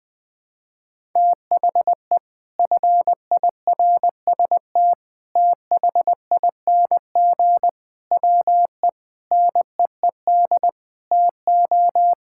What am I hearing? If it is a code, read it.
Morse code, heard as a keyed tone, about 20 wpm, 700 Hz. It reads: THE FIRST THING WE NEED TO
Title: Sets of 8 Words - 20wpm